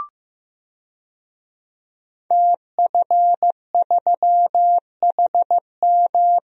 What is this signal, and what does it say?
Morse code, heard as a keyed tone, about 15 wpm, 700 Hz. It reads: TF3HM
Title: DXCC-100 Call Signs - 5 Characters - Repeated 2x - 15wpm